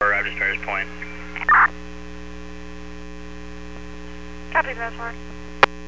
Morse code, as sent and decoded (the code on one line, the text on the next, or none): none